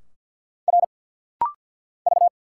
Morse code, heard as a keyed tone, about 49 wpm, 700 Hz. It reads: R V
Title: Single Letters - Rapid-Fire 50wpm